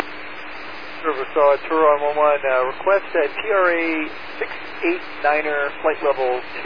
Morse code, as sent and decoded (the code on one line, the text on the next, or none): none